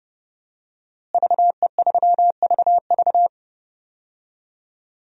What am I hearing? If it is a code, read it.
Morse code, heard as a keyed tone, about 30 wpm, 700 Hz. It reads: VE3VV
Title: DXCC-100 Call Signs - 5 Characters - Repeated 3x - 30wpm